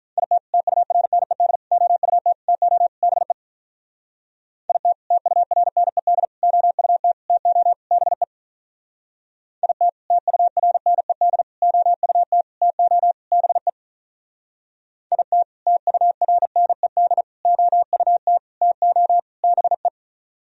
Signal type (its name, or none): Morse code